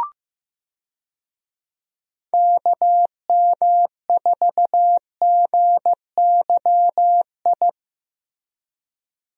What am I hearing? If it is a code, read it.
Morse code, heard as a keyed tone, about 16 wpm, 700 Hz. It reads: KM4GYI
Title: USA Call Signs - 6 Characters - Repeated 2x - 15wpm